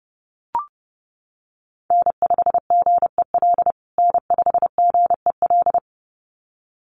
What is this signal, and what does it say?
Morse code, heard as a keyed tone, about 30 wpm, 700 Hz. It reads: N5GEL N5GEL
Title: USA Call Signs Repeated 2x 30wpm